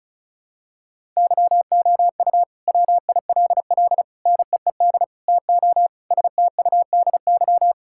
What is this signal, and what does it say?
Morse code, heard as a keyed tone, about 36 wpm, 700 Hz. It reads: YOU WILL NEED TO STUDY
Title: Sentences from Top 1000 Words - Review Part 5 - 35wpm